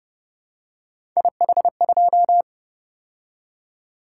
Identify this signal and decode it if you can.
Morse code, heard as a keyed tone, about 31 wpm, 700 Hz. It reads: IH2